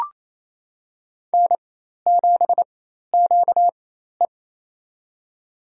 Morse code, heard as a keyed tone, about 28 wpm, spaced as for 13 wpm, 700 Hz. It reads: N7QE
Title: USA Call Signs - 4 Characters - 4x Character Spacing - 28wpm